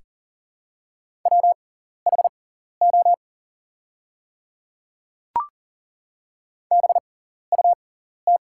Morse code, heard as a keyed tone, about 40 wpm, spaced as for 14 wpm, 700 Hz. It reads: WHO BUT
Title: 6x Character Spacing - Top 100 Words - 40wpm